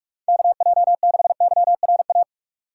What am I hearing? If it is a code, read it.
Morse code, heard as a keyed tone, about 46 wpm, 700 Hz. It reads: KJ6YRA
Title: USA Call Signs - 2x Character Spacing - 45wpm